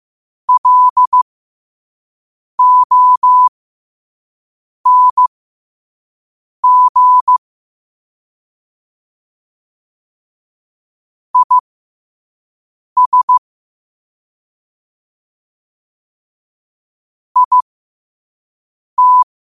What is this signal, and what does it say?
Morse code, heard as a keyed tone, about 15 wpm, spaced as for 5 wpm, 1000 Hz. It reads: LONG IS IT